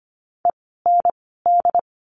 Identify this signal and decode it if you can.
Morse code, heard as a keyed tone, about 25 wpm, spaced as for 16 wpm, 700 Hz. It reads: END
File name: course / 25:10wpm / 094 Lesson 16f - Letter c - Words 25wpm